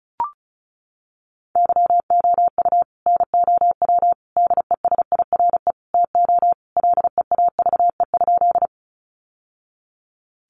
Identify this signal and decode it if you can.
Morse code, heard as a keyed tone, about 34 wpm, 700 Hz. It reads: YOU NOW DESIRE TO LEAVE?
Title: Sentences from Top 1100 Words 35wpm